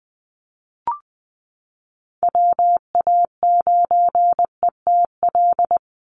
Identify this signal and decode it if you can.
Morse code, heard as a keyed tone, about 20 wpm, 700 Hz. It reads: WA9ETL
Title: USA Call Signs 20wpm